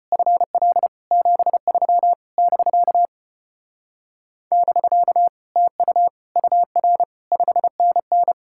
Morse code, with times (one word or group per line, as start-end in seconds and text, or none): 0.1-0.9 FL
1.1-2.1 73
2.4-3.1 <BK>
4.5-5.3 <BK>
5.6-6.1 TU
6.4-7.0 UR
7.3-8.3 5NN